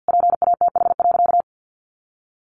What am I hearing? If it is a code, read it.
Morse code, heard as a keyed tone, about 50 wpm, 700 Hz. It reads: PATH.
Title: A Sound Of Thunder 50wpm